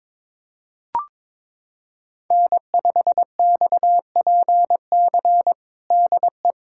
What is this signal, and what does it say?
Morse code, heard as a keyed tone, about 22 wpm, 700 Hz. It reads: N5XPC DE
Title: USA Callsign DE Callsign - 5 Characters - 22wpm